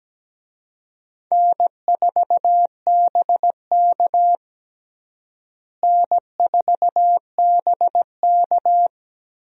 Morse code, heard as a keyed tone, about 17 wpm, 700 Hz. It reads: N4BK N4BK
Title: USA Call Signs - 4 Characters - Repeated 2x - 17wpm